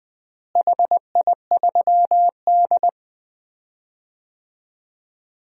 Morse code, heard as a keyed tone, about 20 wpm, 700 Hz. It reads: HI3D
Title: DXCC-100 Call Signs - 4 Characters - Repeated 2x - 20wpm